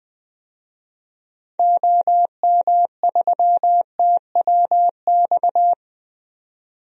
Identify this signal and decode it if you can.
Morse code, heard as a keyed tone, about 20 wpm, 700 Hz. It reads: OM3TWX
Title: DXCC-100 Call Signs - 6 Characters - 20wpm